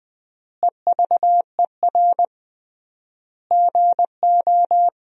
Morse code, heard as a keyed tone, about 20 wpm, 700 Hz. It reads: EVER GO
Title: 3x Word-Spacing - Sentences from Top 400 Words - 20wpm